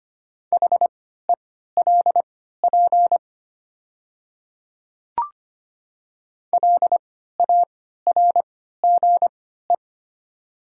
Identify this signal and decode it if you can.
Morse code, heard as a keyed tone, about 25 wpm, spaced as for 14 wpm, 690 Hz. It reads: HELP LARGE
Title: 3x Character Spacing - Top 500 Words - 25wpm